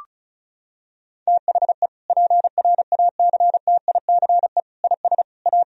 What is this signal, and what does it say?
Morse code, heard as a keyed tone, about 36 wpm, 700 Hz. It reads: THE PRACTICE IS A
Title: Sentences from Top 1000 Words - Review Part 4 - 35wpm